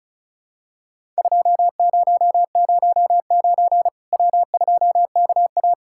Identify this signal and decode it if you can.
Morse code, heard as a keyed tone, about 36 wpm, 700 Hz. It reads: 2009 W2KA
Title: Worked All Europe DX Contest - QTC Exchange - 35wpm